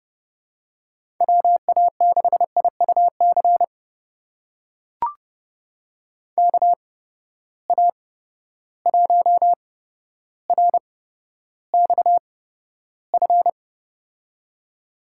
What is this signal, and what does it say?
Morse code, heard as a keyed tone, about 32 wpm, spaced as for 13 wpm, 700 Hz. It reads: WA6IUC KA1RXF
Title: USA Call Signs - 6 Characters - 8x Character Spacing - 30wpm